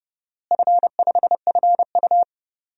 Morse code, heard as a keyed tone, about 30 wpm, 700 Hz. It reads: F5FU
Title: DXCC-100 Call Signs - 4 Characters - Repeated 2x - 30wpm